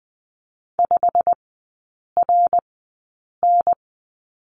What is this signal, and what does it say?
Morse code, heard as a keyed tone, about 20 wpm, spaced as for 8 wpm, 700 Hz. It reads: 5RN
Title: 087 Lesson 15e - Number 5 - Three Characters 20wpm